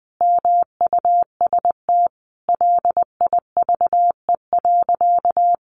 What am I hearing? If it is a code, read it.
Morse code, heard as a keyed tone, about 20 wpm, 700 Hz. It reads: MUST LIVE.
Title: Sentences from Top 800 Words 20wpm